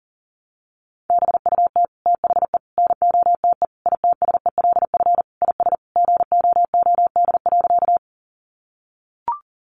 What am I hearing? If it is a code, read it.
Morse code, heard as a keyed tone, about 40 wpm, 700 Hz. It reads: BUT THE NOTE ITSELF IS GOOD.
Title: Sentences from Top 800 Words 40wpm